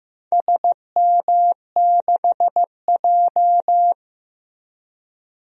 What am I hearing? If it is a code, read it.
Morse code, heard as a keyed tone, about 15 wpm, 700 Hz. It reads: SM6J
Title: DXCC-100 Call Signs - 4 Characters - Repeated 3x - 15wpm